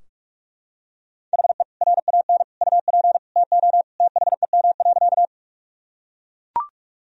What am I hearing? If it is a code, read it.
Morse code, heard as a keyed tone, about 42 wpm, 700 Hz. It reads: HE RAN UP TO THEM.
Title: Sentences from Top 1000 Words - Review Part 3 - 45wpm